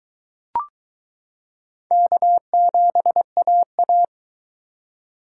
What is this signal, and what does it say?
Morse code, heard as a keyed tone, about 22 wpm, 700 Hz. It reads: K7AA